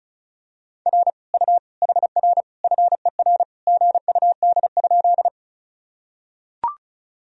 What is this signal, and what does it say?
Morse code, heard as a keyed tone, about 36 wpm, 700 Hz. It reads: R U HR FER GUD?
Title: Sentences from Top 200 Words and Abbreviations 35wpm